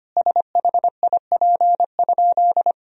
Morse code, heard as a keyed tone, about 25 wpm, 700 Hz. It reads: SHIP?
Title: Sentences from Top 400 Words and Abbreviations 25wpm